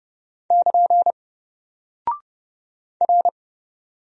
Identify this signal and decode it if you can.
Morse code, heard as a keyed tone, about 30 wpm, 700 Hz. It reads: <KN> R
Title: Single Character (Letter:Number:Punctuation:Prosign) 30wpm